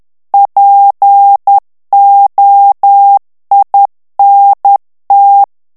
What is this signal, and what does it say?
Morse code, heard as a keyed tone, about 10 wpm, 800 Hz. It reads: POINT
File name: FRANCAIS TEXTE CLAIR 2 VITESSE 10 MOTS MINUTES